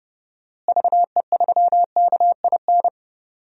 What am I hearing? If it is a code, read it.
Morse code, heard as a keyed tone, about 31 wpm, 700 Hz. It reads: VE3KIN
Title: DXCC-100 Call Signs - 6 Characters - Repeated 3x - 30wpm